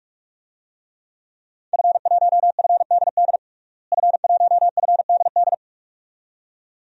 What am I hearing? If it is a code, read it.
Morse code, heard as a keyed tone, about 46 wpm, 700 Hz. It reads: F1FDD F1FDD
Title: DXCC-100 Call Signs - 5 Characters - Repeated 2x - 45wpm